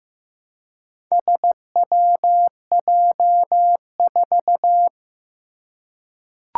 Morse code, heard as a keyed tone, about 15 wpm, 700 Hz. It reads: SWJ4
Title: Four Challenging Letter-Number Combinations 15wpm